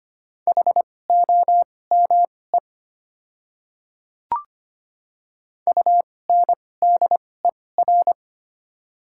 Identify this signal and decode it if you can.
Morse code, heard as a keyed tone, about 25 wpm, spaced as for 18 wpm, 690 Hz. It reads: HOME UNDER
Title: 2x Character Spacing - Top 200 Words - 25wpm